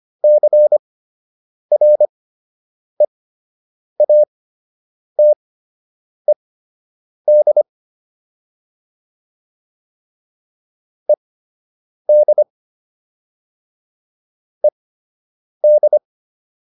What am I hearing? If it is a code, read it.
Morse code, heard as a keyed tone, about 25 wpm, spaced as for 8 wpm, 600 Hz. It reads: CREATED ED ED